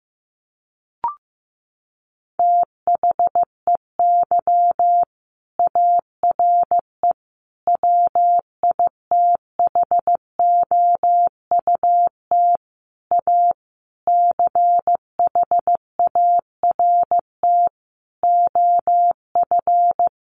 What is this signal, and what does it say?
Morse code, heard as a keyed tone, about 15 wpm, 700 Hz. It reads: THEY ARE WITHOUT A CHART OF
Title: Sentences from Top 1300 Words 15wpm